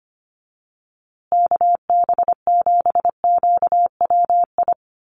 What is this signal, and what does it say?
Morse code, heard as a keyed tone, about 25 wpm, 700 Hz. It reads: KB7QWI
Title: USA Call Signs Repeated 3x 25wpm